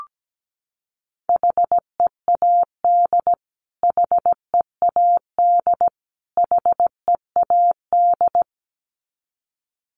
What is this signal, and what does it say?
Morse code, heard as a keyed tone, about 17 wpm, 700 Hz. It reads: HEAD HEAD HEAD
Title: Top 500 Words Repeated 17wpm